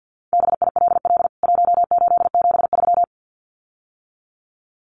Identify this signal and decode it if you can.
Morse code, heard as a keyed tone, about 50 wpm, 700 Hz. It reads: BILL 1873
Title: CWT Exchange 50wpm